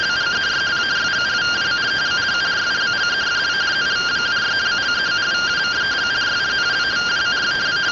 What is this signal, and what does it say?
Two-tone teleprinter signal, about 170 Hz shift.